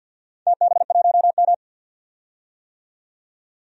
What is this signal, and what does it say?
Morse code, heard as a keyed tone, about 51 wpm, 700 Hz. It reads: TB1M
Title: DXCC-100 Call Signs - 4 Characters - 50wpm